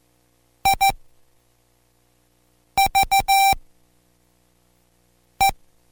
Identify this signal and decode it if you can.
Morse code, heard as a keyed tone, about 14 wpm, spaced as for 4 wpm, 780 Hz. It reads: IVE